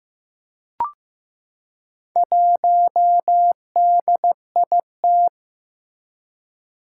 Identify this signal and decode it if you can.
Morse code, heard as a keyed tone, about 15 wpm, 700 Hz. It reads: 1DIT
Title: Four Challenging Letter-Number Combinations 15wpm